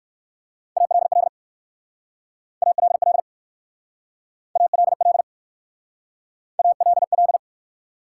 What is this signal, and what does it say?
Morse code, heard as a keyed tone, about 55 wpm, 700 Hz. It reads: ALL ALL ALL ALL
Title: Speed-Racing Top 180 QSO Elements 45wpm